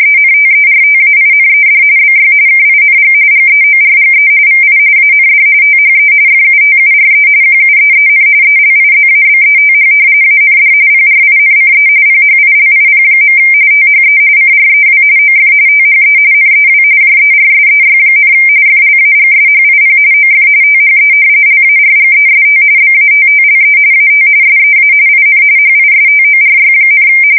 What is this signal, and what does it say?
RTTY, about 45.45 baud. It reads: ENTERTAINED BY KING GEORGE VI AND THE FUTURE QUEEN ELIZABETH AND PRINCE PHILIP. THE CREW OF THE NEW JERSEY, FLAGSHIP OF THE LARGEST BATTLE AT SEA IN HISTORY,
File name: fsk170 / battleshipnj